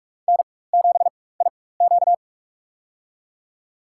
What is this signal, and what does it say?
Morse code, heard as a keyed tone, about 42 wpm, spaced as for 20 wpm, 700 Hz. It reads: N7IQ
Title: USA Call Signs - 4 Characters - 4x Character Spacing - 45wpm